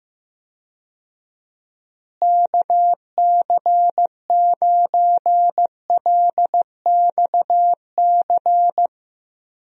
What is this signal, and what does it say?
Morse code, heard as a keyed tone, about 15 wpm, 700 Hz. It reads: KC9LXC
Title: USA Call Signs - 6 Characters - Repeated 2x - 15wpm